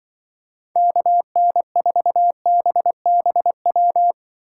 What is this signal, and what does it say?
Morse code, heard as a keyed tone, about 24 wpm, 700 Hz. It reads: KN4BBW
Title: Speed Racing - USA 2x3 Call Signs 20wpm